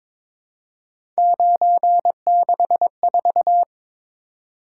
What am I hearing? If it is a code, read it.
Morse code, heard as a keyed tone, about 22 wpm, 700 Hz. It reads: 964
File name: practice / 22wpm / Three Letter-Number Combinations - 22wpm